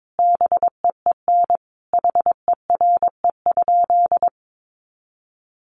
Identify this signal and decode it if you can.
Morse code, heard as a keyed tone, about 22 wpm, 700 Hz. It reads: BEEN HERE?
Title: Sentences from Top 200 Words 22wpm